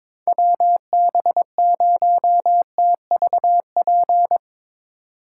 Morse code, heard as a keyed tone, about 22 wpm, 700 Hz. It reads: WB0TVP
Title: Speed Racing - USA 2x3 Call Signs 22wpm